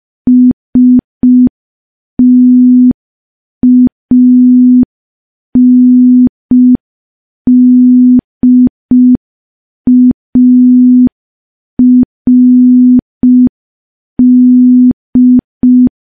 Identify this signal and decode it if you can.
Morse code, heard as a keyed tone, about 5 wpm, 250 Hz. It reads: STANDARD
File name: morse / morse-training-05WPM